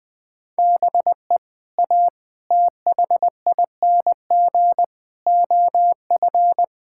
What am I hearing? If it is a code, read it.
Morse code, heard as a keyed tone, about 20 wpm, 700 Hz. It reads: BE A THING OF
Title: Sets of 6 Words - 20wpm